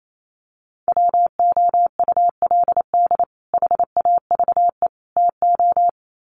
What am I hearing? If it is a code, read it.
Morse code, heard as a keyed tone, about 28 wpm, 700 Hz. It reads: WOULD HAVE TO
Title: Sets of 3 Words 28wpm